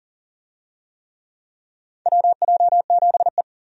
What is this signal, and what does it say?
Morse code, heard as a keyed tone, about 40 wpm, 700 Hz. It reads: WJ7E
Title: USA Call Signs - 4 Characters - 40wpm